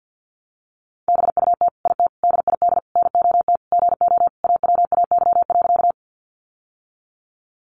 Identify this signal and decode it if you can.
Morse code, heard as a keyed tone, about 50 wpm, 700 Hz. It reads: BUT IT DID NOT GO AWAY.
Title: Sentences from Top 600 Words 50wpm